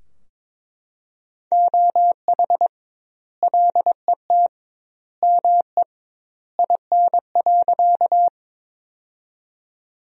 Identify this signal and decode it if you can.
Morse code, heard as a keyed tone, about 22 wpm, 700 Hz. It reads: OH LET ME IN.
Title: 2x Word-Spacing - Sentences from Top 500 Words - 22wpm